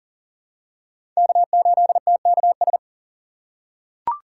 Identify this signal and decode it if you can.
Morse code, heard as a keyed tone, about 39 wpm, 700 Hz. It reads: K8TKS
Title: USA Call Signs - 5 Characters - 3x Character Spacing - 40wpm